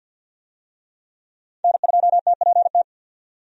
Morse code, heard as a keyed tone, about 47 wpm, 700 Hz. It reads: N2TPT
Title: USA Call Signs - 5 Characters - 50wpm